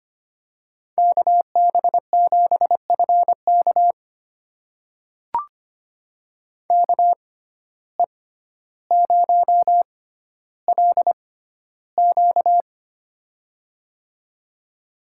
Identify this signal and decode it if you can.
Morse code, heard as a keyed tone, about 25 wpm, spaced as for 13 wpm, 700 Hz. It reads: KB7FK KE0LQ
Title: USA Call Signs - 5 Characters - 6x Character Spacing - 25wpm